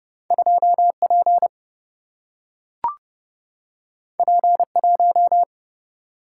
Morse code, heard as a keyed tone, about 30 wpm, 700 Hz. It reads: 2P P1
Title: Single-Letter Single-Number 30wpm